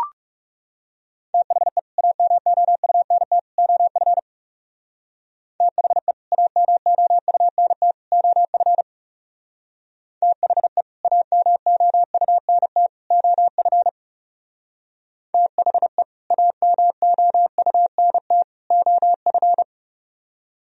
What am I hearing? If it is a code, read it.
Morse code, heard as a keyed tone, about 34 wpm, 700 Hz. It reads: THE AMOUNT OF THE AMOUNT OF THE AMOUNT OF THE AMOUNT OF
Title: Speed-Racing - Sets of 3 Words 30wpm